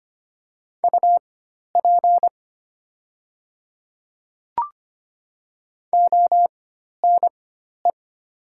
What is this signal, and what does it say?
Morse code, heard as a keyed tone, about 25 wpm, spaced as for 12 wpm, 690 Hz. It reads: UP ONE
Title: 4x Character Spacing - Top 100 Words - 25wpm